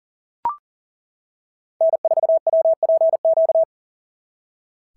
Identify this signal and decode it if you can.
Morse code, heard as a keyed tone, about 40 wpm, 650 Hz. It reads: N4WPQ
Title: SOTA QSO Practice 40wpm